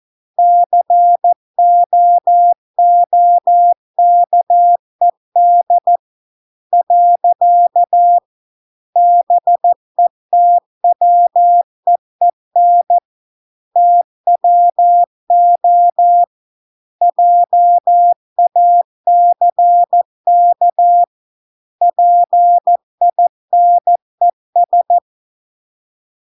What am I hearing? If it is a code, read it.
Morse code, heard as a keyed tone, about 14 wpm, 700 Hz. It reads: COOKED . BETWEEN TWO JACK PINES